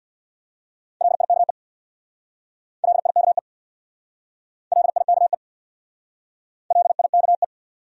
Morse code, heard as a keyed tone, about 62 wpm, 700 Hz. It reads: LIKE LIKE LIKE LIKE
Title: Speed-Racing Top 500 Words 50wpm